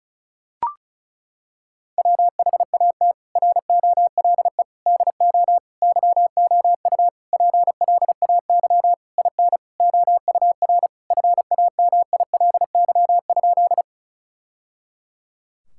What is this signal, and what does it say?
Morse code, heard as a keyed tone, about 34 wpm, 700 Hz. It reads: WHAT ROLE DO YOU PLAY IN OUR FAMILY?
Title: Sentences from Top 1000 Words - Review Part 6 - 35wpm